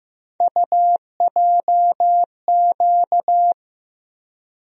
Morse code, heard as a keyed tone, about 15 wpm, 700 Hz. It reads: UJQ
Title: USA Call Sign Postfixes - 3 Letters 15wpm